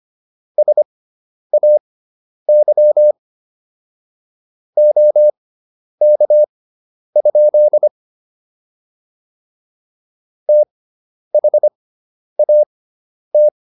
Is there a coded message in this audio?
Yes: Morse code from 0.6 to 13.5 s, heard as a keyed tone, about 25 wpm, spaced as for 10 wpm, 600 Hz. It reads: SAY OK? THAT